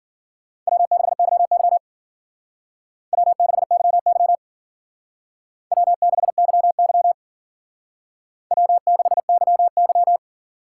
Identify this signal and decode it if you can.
Morse code, heard as a keyed tone, about 51 wpm, 700 Hz. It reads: W6YY W6YY W6YY W6YY